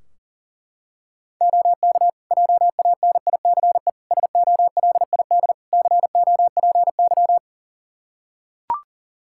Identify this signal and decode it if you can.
Morse code, heard as a keyed tone, about 40 wpm, 700 Hz. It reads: OK JANICE SOLID COPY
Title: Practice QSO - Exchange 3 - More Info - 40wpm